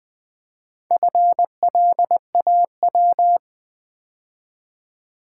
Morse code, heard as a keyed tone, about 20 wpm, 700 Hz. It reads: FLAW
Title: Build A Word - 2 Letters At A Time - 6-Letter Words - 20wpm